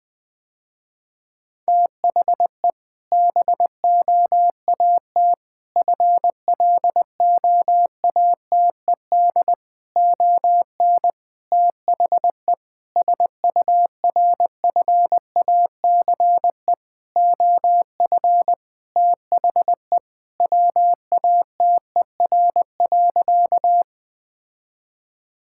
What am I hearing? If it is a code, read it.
Morse code, heard as a keyed tone, about 21 wpm, 700 Hz. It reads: THE BOAT FLOATED ON THE SURFACE OF THE WATER.
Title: Sentences with 4-Word Phrases - Spoken Beginning - 20wpm